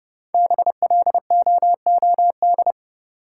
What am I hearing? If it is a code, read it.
Morse code, heard as a keyed tone, about 30 wpm, 700 Hz. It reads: BLOOD
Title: Sentences from Top 1000 Words - Review Part 2 - New Word Drill - 30wpm